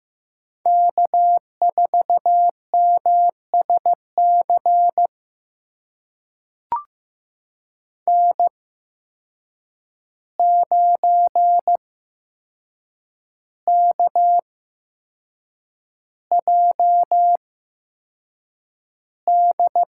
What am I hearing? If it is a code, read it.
Morse code, heard as a keyed tone, about 15 wpm, spaced as for 6 wpm, 700 Hz. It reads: K4MSC N9KJD
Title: USA Call Signs - 5 Characters - 8x Character Spacing - 15wpm